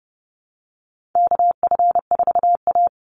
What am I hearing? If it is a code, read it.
Morse code, heard as a keyed tone, about 30 wpm, 700 Hz. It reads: KF4A
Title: Call Signs Repeated 2x - 30wpm